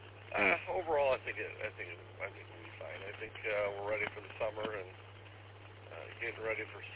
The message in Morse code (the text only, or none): none